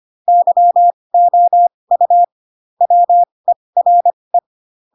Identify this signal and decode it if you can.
Morse code, heard as a keyed tone, about 25 wpm, spaced as for 20 wpm, 700 Hz. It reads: YOU WERE
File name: Pinocchio / Pinocchio ch7 15 25 ik1hge